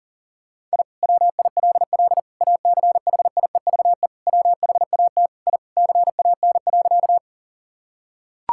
Morse code, heard as a keyed tone, about 40 wpm, 690 Hz. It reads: I WILL ACHIEVE WHAT I CAN.
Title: Sentences from Top 1000 Words - Review Part 6 - 40wpm